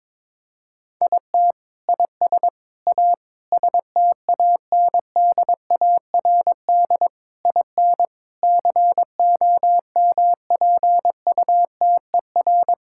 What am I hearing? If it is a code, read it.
Morse code, heard as a keyed tone, about 22 wpm, 700 Hz. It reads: IT IS A STANDARD IN COMPUTER
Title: Sentences from Top 1000 Words - Review Part 6 - 22wpm